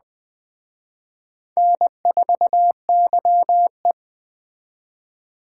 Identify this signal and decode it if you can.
Morse code, heard as a keyed tone, about 21 wpm, 700 Hz. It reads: N4YE
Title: Speed Racing - USA 1x2 Call Signs 17wpm